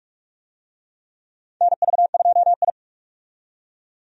Morse code, heard as a keyed tone, about 46 wpm, 700 Hz. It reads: NV2I
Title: USA Call Signs - 4 Characters - 45wpm